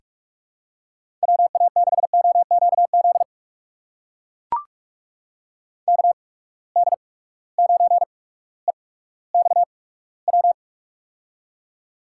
Morse code, heard as a keyed tone, about 43 wpm, spaced as for 19 wpm, 700 Hz. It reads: WA6OQZ KD9EXW
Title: USA Call Signs - 6 Characters - 8x Character Spacing - 45wpm